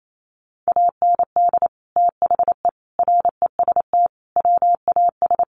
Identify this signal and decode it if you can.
Morse code, heard as a keyed tone, about 28 wpm, 700 Hz. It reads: AND THE REST WAS
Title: Sentences from Top 1000 Words 28wpm